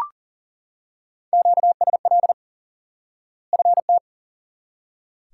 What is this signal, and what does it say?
Morse code, heard as a keyed tone, about 40 wpm, 700 Hz. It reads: QSL FT
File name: course / instant-qso-element-course-3 / IQE Course 3 - Lesson 89 - YAGI - Two Elements - 40wpm